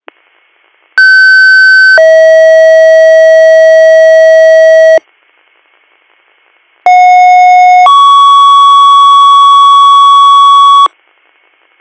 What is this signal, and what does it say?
Two-tone page, heard as a one-second tone, then a three-second tone, about 1550 then 640 Hz, about 730 then 1120 Hz. It is heard from 1.0 to 10.9 s.
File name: 200- .0-call 8117